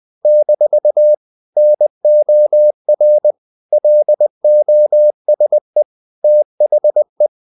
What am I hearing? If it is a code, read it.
Morse code, heard as a keyed tone, about 20 wpm, 600 Hz. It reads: - NOR LOSE THE
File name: CW / Practice / If20